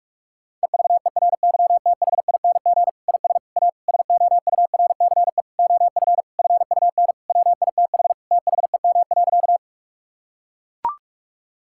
Morse code, heard as a keyed tone, about 41 wpm, 700 Hz. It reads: EVERYTHING IS A SOURCE OF FUN WITH THEM.
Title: Sentences from Top 1000 Words - Review Part 5 - 45wpm